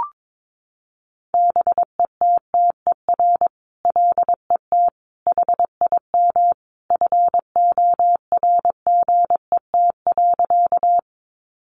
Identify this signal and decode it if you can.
Morse code, heard as a keyed tone, about 22 wpm, 700 Hz. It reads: BETTER LET HIM FORGET.
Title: Sentences from Top 1200 Words 22wpm